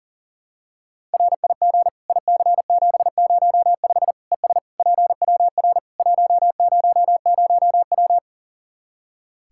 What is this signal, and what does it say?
Morse code, heard as a keyed tone, about 40 wpm, 700 Hz. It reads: RIG IC705 ES PWR 100W
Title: Practice QSO - Exchange 2 - Info - 40wpm